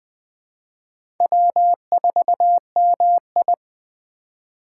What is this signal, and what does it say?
Morse code, heard as a keyed tone, about 20 wpm, 700 Hz. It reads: W4MI
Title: USA Call Signs - 4 Characters - 3x Character Spacing - 20wpm